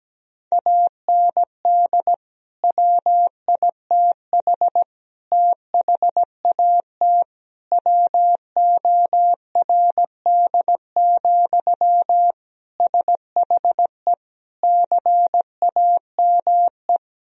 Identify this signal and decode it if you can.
Morse code, heard as a keyed tone, about 17 wpm, 700 Hz. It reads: AND WITH THAT WORD, SHE CAME
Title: Sentences from Top 1000 Words - Review Part 1 - 17wpm